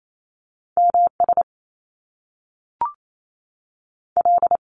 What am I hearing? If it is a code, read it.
Morse code, heard as a keyed tone, about 28 wpm, 700 Hz. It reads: M S L
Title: Top Two Letter Combinations in Words 28wpm